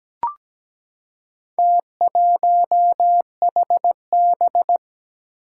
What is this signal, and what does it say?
Morse code, heard as a keyed tone, about 17 wpm, 700 Hz. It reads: T1HB
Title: Four Challenging Letter-Number Combinations 17wpm